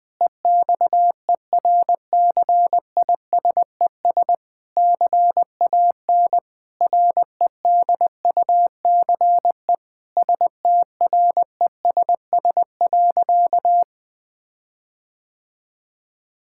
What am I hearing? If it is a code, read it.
Morse code, heard as a keyed tone, about 20 wpm, 700 Hz. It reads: EXERCISES CAN REDUCE STRESS.
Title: Difficult Sentences with Hints - 20wpm